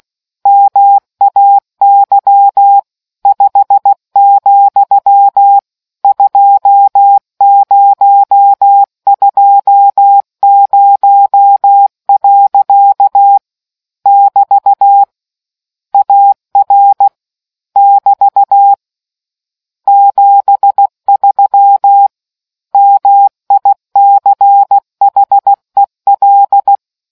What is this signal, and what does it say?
Morse code, heard as a keyed tone, about 16 wpm, 800 Hz. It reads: MAY 5, 2020. <BT> AR <BT> 73 MICHEL